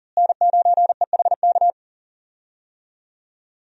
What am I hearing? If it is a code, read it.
Morse code, heard as a keyed tone, about 40 wpm, 700 Hz. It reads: N9EHK
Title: USA Call Signs - 5 Characters - Repeated 3x - 40wpm